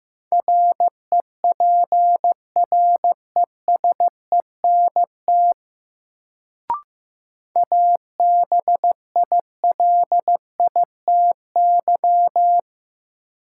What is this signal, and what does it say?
Morse code, heard as a keyed tone, about 15 wpm, 700 Hz. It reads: REPRESENT ABILITY
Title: Sentences from Top 1200 Words - New Word Drill Alt - 15wpm